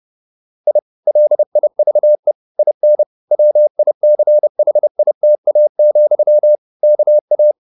Morse code, heard as a keyed tone, about 30 wpm, 600 Hz. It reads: I LIVE IN WICHITA, KA